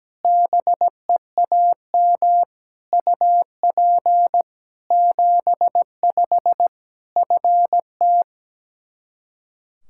Morse code, heard as a keyed tone, about 17 wpm, 700 Hz. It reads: BEAM UP 75 FT
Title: Practice QSO - Exchange 2 - Info - 17wpm